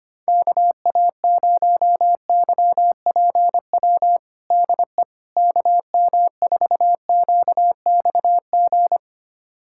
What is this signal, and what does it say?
Morse code, heard as a keyed tone, about 25 wpm, 700 Hz. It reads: KA0YPW DE KM4QXG
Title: USA Callsign DE Callsign - 6 Characters - 25wpm